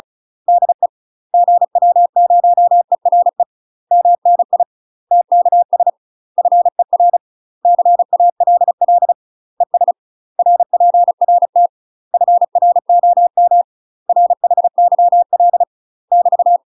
Morse code, heard as a keyed tone, about 34 wpm, 700 Hz. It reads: DE GW0ERE MNI TKS FER CALL ES RPRT FROM RHYL <BT>
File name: UK MASTER QSOs TRAINING / 35WPM / UK MASTER QSOs 0027